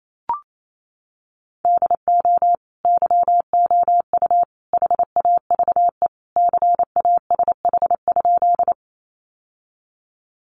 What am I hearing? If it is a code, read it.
Morse code, heard as a keyed tone, about 28 wpm, 700 Hz. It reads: DO YOU HAVE CASH?
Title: Sentences from Top 700 Words 28wpm